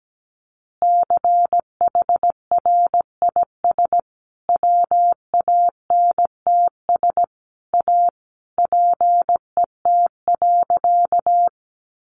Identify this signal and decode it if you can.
Morse code, heard as a keyed tone, about 17 wpm, 700 Hz. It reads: CHRIS WANTS A PET.
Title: Easy Phrases 17wpm